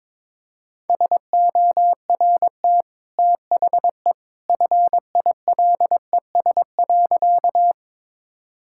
Morse code, heard as a keyed tone, about 22 wpm, 700 Hz. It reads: SORT THE FILES.
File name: practice / 22wpm / Easy Sentences with Hints - 22wpm